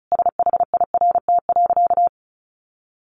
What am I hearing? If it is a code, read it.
Morse code, heard as a keyed tone, about 34 wpm, 700 Hz. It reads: SHIRT.
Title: Additional Easy Phrases 35wpm